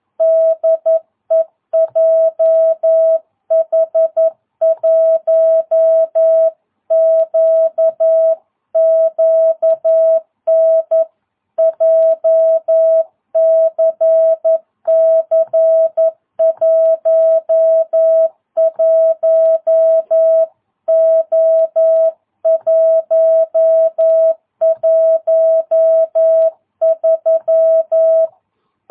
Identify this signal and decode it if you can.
Morse code, heard as a keyed tone, about 11 wpm, 660 Hz. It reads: DE JH1QQN JCC 11O113
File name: jh8cbh / 2022cw